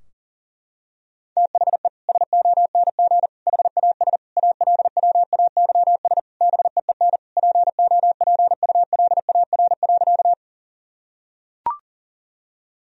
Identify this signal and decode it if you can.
Morse code, heard as a keyed tone, about 40 wpm, 700 Hz. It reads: THE SONG HAS ALWAYS BEEN POPULAR.
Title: Sentences from Top 1000 Words - Review Part 3 - 40wpm